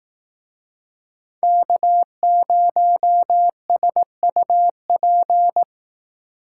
Morse code, heard as a keyed tone, about 18 wpm, 700 Hz. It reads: K0SUP
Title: Speed Racing - USA 1x3 Call Signs 15wpm